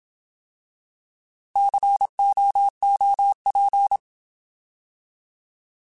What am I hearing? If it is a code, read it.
Morse code, heard as a keyed tone, about 27 wpm, 800 Hz. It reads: COOP
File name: Alphabetical